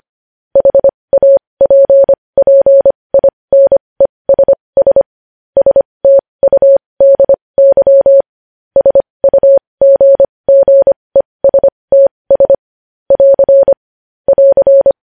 Morse code, heard as a keyed tone, about 25 wpm, spaced as for 20 wpm, 550 Hz. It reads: HAPPINESS STUDY SUGGESTS <AR> <AR>
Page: news-headlines-in-morse-code-at-20-wpm